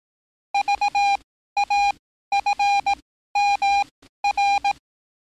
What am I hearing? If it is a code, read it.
Morse code, heard as a keyed tone, about 18 wpm, spaced as for 13 wpm, 810 Hz. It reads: VAFMR
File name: training / NAVPERS 11049to11060 / NAVPERS 11054 RA RB